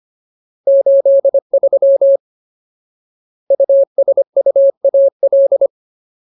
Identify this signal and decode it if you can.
Morse code, heard as a keyed tone, about 25 wpm, 550 Hz. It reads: 83 USUAL
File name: Ultimate Word List / 25WPM Ultimate